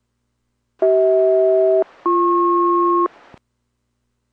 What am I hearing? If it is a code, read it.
Aircraft SELCAL, heard as two pairs of tones. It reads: CH-BP